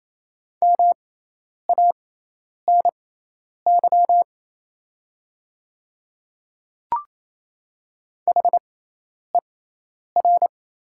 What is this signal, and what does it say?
Morse code, heard as a keyed tone, about 27 wpm, spaced as for 10 wpm, 690 Hz. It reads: MANY HER
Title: 6x Character Spacing - Top 100 Words - 28wpm